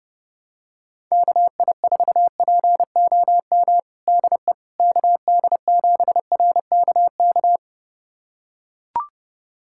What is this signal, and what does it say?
Morse code, heard as a keyed tone, about 30 wpm, 700 Hz. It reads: KI4POM DE KD7RKK
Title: USA Callsign DE Callsign - 6 Characters - 30wpm